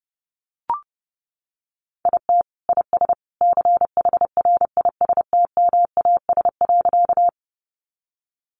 Morse code, heard as a keyed tone, about 30 wpm, 700 Hz. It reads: IT IS CHRISTMAS.
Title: Easy Phrases 30wpm